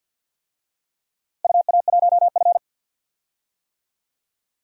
Morse code, heard as a keyed tone, about 47 wpm, 700 Hz. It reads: UA1F
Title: DXCC-100 Call Signs - 4 Characters - Repeated 3x - 50wpm